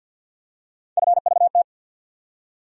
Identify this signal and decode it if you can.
Morse code, heard as a keyed tone, about 52 wpm, 700 Hz. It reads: FVT
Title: USA Call Sign Postfixes - 3 Letters 50wpm